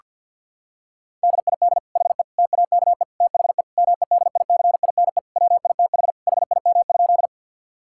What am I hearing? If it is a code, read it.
Morse code, heard as a keyed tone, about 50 wpm, 700 Hz. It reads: DID HE TAKE THE MEDICINE WITH HIM?
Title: Sentences from Top 1000 Words - Review Part 6 - 50wpm